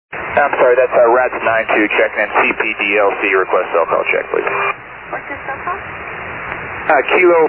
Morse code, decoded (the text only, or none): none